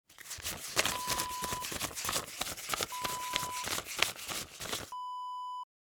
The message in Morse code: none